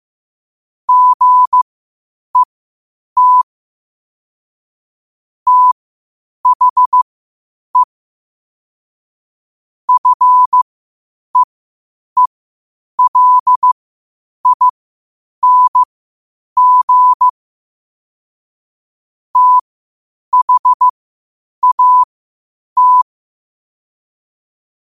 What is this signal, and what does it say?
Morse code, heard as a keyed tone, about 15 wpm, spaced as for 8 wpm, 1000 Hz. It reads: GET THE FEELING THAT
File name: qotdfiles / 8wpm